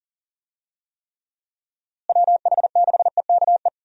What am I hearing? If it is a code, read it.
Morse code, heard as a keyed tone, about 40 wpm, 700 Hz. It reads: WH6EKE
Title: USA Call Signs - 6 Characters - Repeated 2x - 40wpm